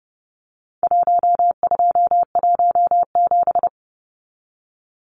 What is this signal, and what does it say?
Morse code, heard as a keyed tone, about 30 wpm, 700 Hz. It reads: 1217
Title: Numbers - Contest Style - 4-Digit 30wpm